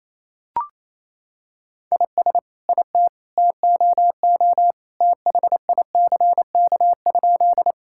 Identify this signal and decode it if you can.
Morse code, heard as a keyed tone, about 28 wpm, 700 Hz. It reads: IS IT TOO THICK?
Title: Sentences from Top 1600 Words 28wpm